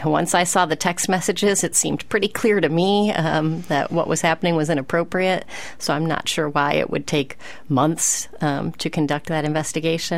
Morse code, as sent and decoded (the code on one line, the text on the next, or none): none